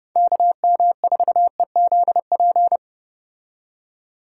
Morse code, heard as a keyed tone, about 32 wpm, 700 Hz. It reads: KM4EZP